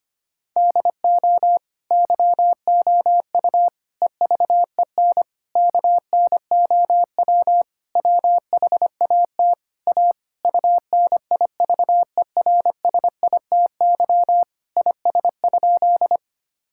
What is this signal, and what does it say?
Morse code, heard as a keyed tone, about 25 wpm, 700 Hz. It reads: DO YOU EVEN KNOW WHAT A UNIVERSITY IS?
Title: Sentences from Top 1000 Words - Review Part 5 - 25wpm